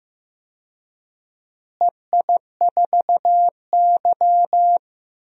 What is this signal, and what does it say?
Morse code, heard as a keyed tone, about 15 wpm, 700 Hz. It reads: EI4Y
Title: DXCC-100 Call Signs - 4 Characters - Repeated 2x - 15wpm